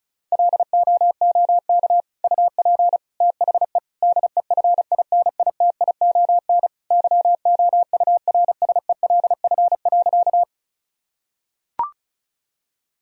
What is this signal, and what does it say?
Morse code, heard as a keyed tone, about 36 wpm, 700 Hz. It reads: LOOK UP THE DEFINITION YOURSELF.
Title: Sentences from Top 1000 Words - Review Part 4 - 35wpm